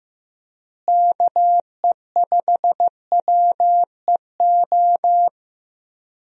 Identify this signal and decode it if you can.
Morse code, heard as a keyed tone, about 16 wpm, 700 Hz. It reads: KE5WEO
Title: USA Call Signs - 6 Characters - Repeated 2x - 15wpm